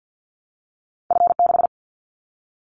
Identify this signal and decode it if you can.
Morse code, heard as a keyed tone, about 50 wpm, 700 Hz. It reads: F6
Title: Callsign Prefixes for Top 50 DXCC - 50wpm